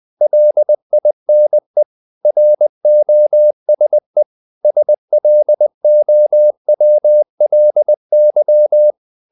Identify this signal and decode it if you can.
Morse code, heard as a keyed tone, about 20 wpm, 600 Hz. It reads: LINE ROSE SLOWLY
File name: the old man and the sea / the old man and the sea part 1 20wpm